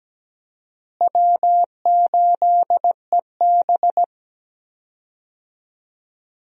Morse code, heard as a keyed tone, about 17 wpm, 700 Hz. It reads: W8EB
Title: USA Call Signs - 4 Characters - Repeated 2x - 17wpm